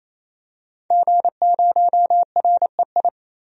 Morse code, heard as a keyed tone, about 28 wpm, 700 Hz. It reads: G0REI
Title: DXCC-100 Call Signs - 5 Characters - 28wpm